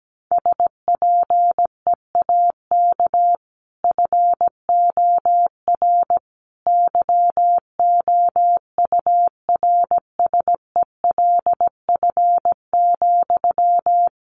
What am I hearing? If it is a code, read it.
Morse code, heard as a keyed tone, about 17 wpm, 700 Hz. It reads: SPEAK FOR YOURSELF,
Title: Sentences from Top 1100 Words 17wpm